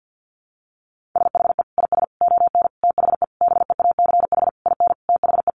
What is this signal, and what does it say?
Morse code, heard as a keyed tone, about 50 wpm, 700 Hz. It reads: SHE IS ON THE BEACH IN THE
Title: Sentences from Top 1500 Words 50wpm